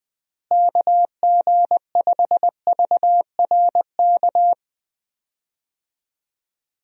Morse code, heard as a keyed tone, about 20 wpm, 700 Hz. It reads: KG5VRK
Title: USA Call Signs - 6 Characters - Repeated 2x - 20wpm